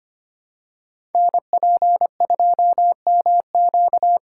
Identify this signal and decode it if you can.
Morse code, heard as a keyed tone, about 25 wpm, 700 Hz. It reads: NP2MQ
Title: Code Groups - 5 Letter:Numbers - 25wpm